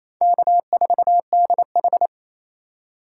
Morse code, heard as a keyed tone, about 27 wpm, 700 Hz. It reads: K4DH
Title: USA Call Signs - 4 Characters - 3x Character Spacing - 28wpm